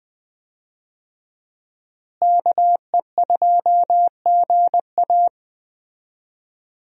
Morse code, heard as a keyed tone, about 21 wpm, 700 Hz. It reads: KE2GA
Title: USA Call Signs - 5 Characters - Repeated 3x - 20wpm